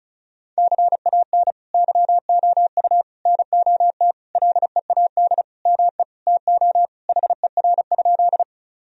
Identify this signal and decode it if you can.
Morse code, heard as a keyed tone, about 36 wpm, 700 Hz. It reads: CAN YOU NOT LEAD ME TO HER?
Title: Sentences from Top 1000 Words - Review Part 2 - 35wpm